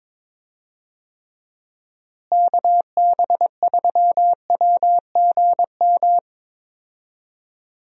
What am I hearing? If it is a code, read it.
Morse code, heard as a keyed tone, about 22 wpm, 700 Hz. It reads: KB3WGM
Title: USA Call Signs - 6 Characters - Repeated 3x - 22wpm